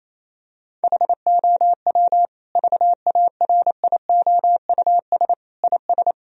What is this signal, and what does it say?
Morse code, heard as a keyed tone, about 28 wpm, 700 Hz. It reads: HOW VARIOUS IS
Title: Sentences from Top 1000 Words - Review Part 4 - 28wpm